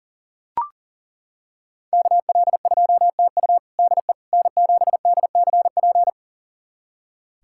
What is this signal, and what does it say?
Morse code, heard as a keyed tone, about 40 wpm, 700 Hz. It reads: KL2TU DE N7DCP
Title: USA Callsign DE Callsign - 5 Characters - 40wpm